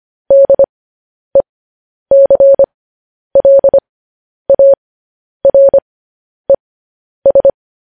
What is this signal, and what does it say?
Morse code, heard as a keyed tone, about 25 wpm, spaced as for 10 wpm, 550 Hz. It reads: DECLARES
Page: news-headlines-in-morse-code-at-10-wpm